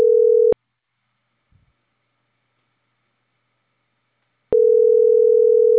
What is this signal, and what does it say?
Ringback tone (North America), 440 + 480 Hz.